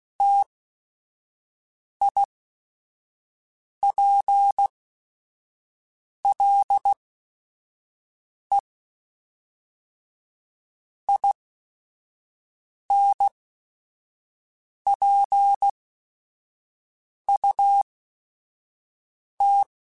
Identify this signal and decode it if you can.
Morse code, heard as a keyed tone, about 16 wpm, spaced as for 5 wpm, 800 Hz. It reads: TIPLE INPUT
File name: Technical Subjects / Technical Subjects-mp3s / AND Gate @5